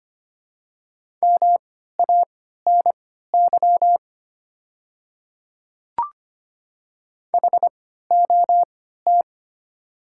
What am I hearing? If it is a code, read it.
Morse code, heard as a keyed tone, about 25 wpm, spaced as for 14 wpm, 690 Hz. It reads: MANY HOT
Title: 3x Character Spacing - Top 100 Words - 25wpm